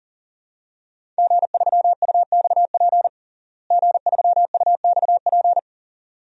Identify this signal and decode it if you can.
Morse code, heard as a keyed tone, about 40 wpm, 700 Hz. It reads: G3UXP G3UXP
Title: DXCC-100 Call Signs - 5 Characters - Repeated 2x - 40wpm